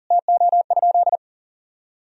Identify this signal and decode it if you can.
Morse code, heard as a keyed tone, about 40 wpm, 700 Hz. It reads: TO?
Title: Sentences from Top 1000 Words - Review Part 4 - 40wpm